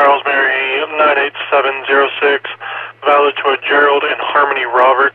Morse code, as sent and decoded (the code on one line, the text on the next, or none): none